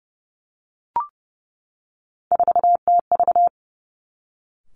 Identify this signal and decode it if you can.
Morse code, heard as a keyed tone, about 30 wpm, 700 Hz. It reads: 4TV
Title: Three Challenging Letter-Number Combinations 30wpm